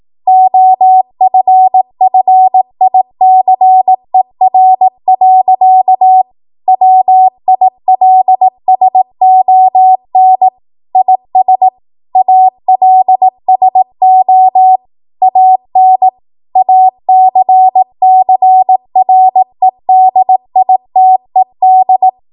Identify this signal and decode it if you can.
Morse code, heard as a keyed tone, about 18 wpm, 750 Hz. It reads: OFFICER. WILSON IS ALSO AN ACCREDITED
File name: Archive / 18 WPM / 18WPM